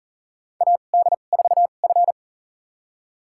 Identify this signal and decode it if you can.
Morse code, heard as a keyed tone, about 40 wpm, spaced as for 29 wpm, 690 Hz. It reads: AD4F
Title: USA Call Signs - 4 Characters - 2x Character Spacing - 40wpm